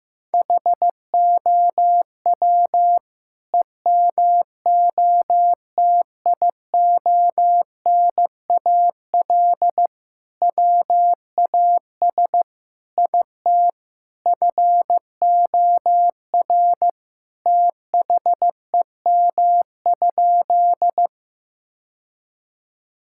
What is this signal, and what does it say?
Morse code, heard as a keyed tone, about 16 wpm, 700 Hz. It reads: HOW EMOTIONAL WAS IT FOR THEM?
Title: Sentences from Top 1000 Words - Review Part 5 - 15wpm